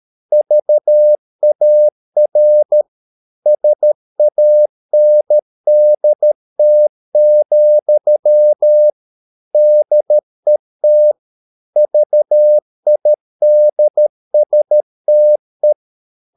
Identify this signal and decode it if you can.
Morse code, heard as a keyed tone, about 13 wpm, 600 Hz. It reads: VAR SANDT, DET VIDSTE
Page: Ferie 13 wpm